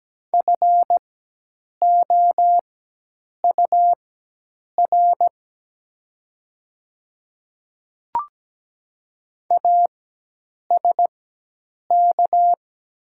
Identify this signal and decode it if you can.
Morse code, heard as a keyed tone, about 17 wpm, spaced as for 8 wpm, 700 Hz. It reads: FOUR ASK
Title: 4x Character Spacing - Top 300 Words - 17wpm